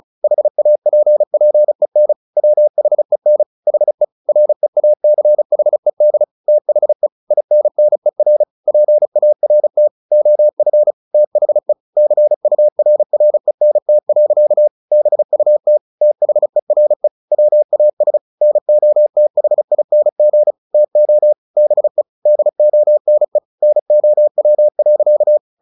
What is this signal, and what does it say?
Morse code, heard as a keyed tone, about 34 wpm, 600 Hz. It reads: HAPPEN WHEN HE REACHED THE INNER PART OF THE CURRENT. BUT THERE WAS NOTHING TO BE DONE NOW.
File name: the old man and the sea 2 35wpm